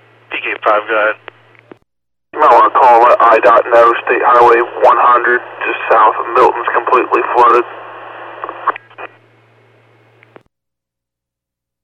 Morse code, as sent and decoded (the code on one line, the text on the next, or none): none